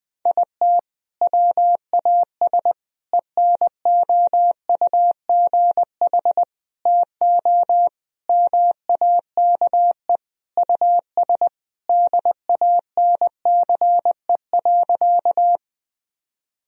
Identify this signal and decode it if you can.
Morse code, heard as a keyed tone, about 21 wpm, 700 Hz. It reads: IT WAS ENOUGH TO MAKE US DANCE.
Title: Sentences from Top 1000 Words - Review Part 4 - 20wpm